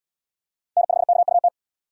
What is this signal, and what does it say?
Morse code, heard as a keyed tone, about 69 wpm, 700 Hz. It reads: N5KUT